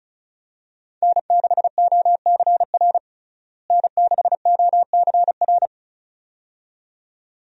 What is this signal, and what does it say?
Morse code, heard as a keyed tone, about 36 wpm, 700 Hz. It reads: N6OCR N6OCR
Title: USA Call Signs - 5 Characters - Repeated 2x - 35wpm